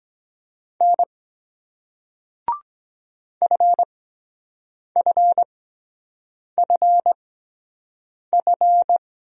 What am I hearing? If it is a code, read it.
Morse code, heard as a keyed tone, about 23 wpm, spaced as for 7 wpm, 700 Hz. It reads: N FFFF